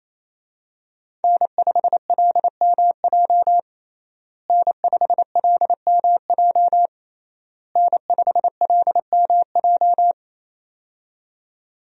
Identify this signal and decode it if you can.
Morse code, heard as a keyed tone, about 28 wpm, 700 Hz. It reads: N5LMJ N5LMJ N5LMJ
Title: USA Call Signs - 5 Characters - Repeated 3x - 28wpm